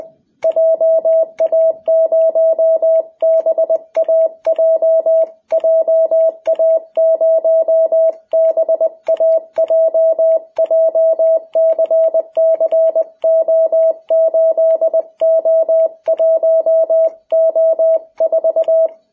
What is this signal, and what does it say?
Morse code, heard as a keyed tone, about 20 wpm, 660 Hz. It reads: JA0BAJ JA0BAJ JCCO8O1O4
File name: jh8cbh / 2021cw